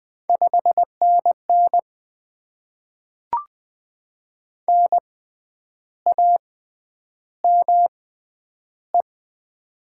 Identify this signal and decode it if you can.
Morse code, heard as a keyed tone, about 20 wpm, spaced as for 9 wpm, 700 Hz. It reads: 5NN NAME
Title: Top 20 QSO Elements - 6x Character Spacing - 20wpm